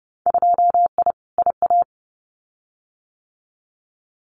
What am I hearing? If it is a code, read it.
Morse code, heard as a keyed tone, about 30 wpm, 700 Hz. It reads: 2I IA
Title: Winter Field Day Exchanges Encore 30wpm